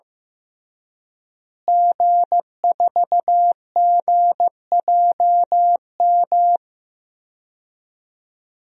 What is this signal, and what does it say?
Morse code, heard as a keyed tone, about 16 wpm, 700 Hz. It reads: G4GJM
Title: DXCC-100 Call Signs - 5 Characters - Repeated 2x - 15wpm